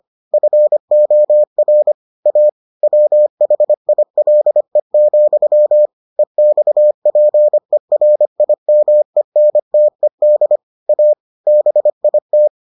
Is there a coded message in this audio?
Yes: Morse code from 0.3 to 12.5 s, heard as a keyed tone, about 25 wpm, 600 Hz. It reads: FOR A WHILE, EXPERIMENTED A BIT